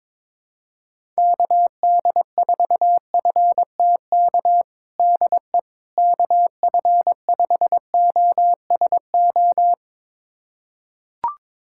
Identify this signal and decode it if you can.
Morse code, heard as a keyed tone, about 22 wpm, 700 Hz. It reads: KD4FTK DE KF5OSO
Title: USA Callsign DE Callsign 22wpm